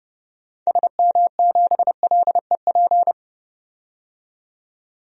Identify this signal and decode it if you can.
Morse code, heard as a keyed tone, about 30 wpm, 700 Hz. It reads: SM7LEP
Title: DXCC-100 Call Signs - 6 Characters - 30wpm